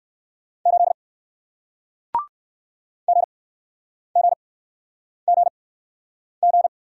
Morse code, heard as a keyed tone, about 60 wpm, spaced as for 9 wpm, 700 Hz. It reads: 8 GGGG